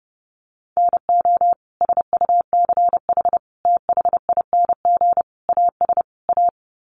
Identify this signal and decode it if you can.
Morse code, heard as a keyed tone, about 30 wpm, 700 Hz. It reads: NO SUCH THING AS A
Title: Sets of 5 Words Encore 30wpm